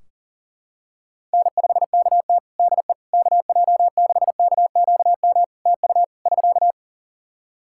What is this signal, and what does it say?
Morse code, heard as a keyed tone, about 40 wpm, 700 Hz. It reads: N5KT DE KJ6KQM TU <SK>
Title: Practice QSO - Exchange 4 - Ending - 40wpm